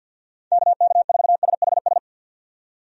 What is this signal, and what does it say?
Morse code, heard as a keyed tone, about 52 wpm, 700 Hz. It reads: KK4SHS
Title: USA Call Signs - 6 Characters - 50wpm